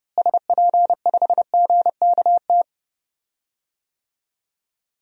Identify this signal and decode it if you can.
Morse code, heard as a keyed tone, about 31 wpm, 700 Hz. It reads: SP5GKT
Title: DXCC-100 Call Signs - 6 Characters - Repeated 3x - 30wpm